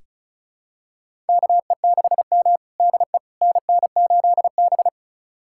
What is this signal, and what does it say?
Morse code, heard as a keyed tone, about 36 wpm, 700 Hz. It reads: KE6M DE NN8B
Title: USA Callsign DE Callsign - 4 Characters - 35wpm